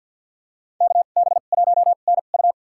Morse code, heard as a keyed tone, about 52 wpm, spaced as for 37 wpm, 700 Hz. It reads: KB1NU
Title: USA Call Signs - 2x Character Spacing - 50wpm